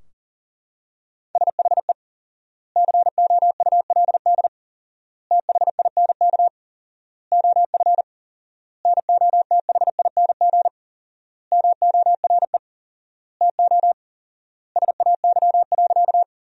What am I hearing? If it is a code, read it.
Morse code, heard as a keyed tone, about 37 wpm, 700 Hz. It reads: SHE COULD THINK OF NOTHING MORE TO SAY.
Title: 4x Word-Spacing - Sentences from Top 500 Words - 40wpm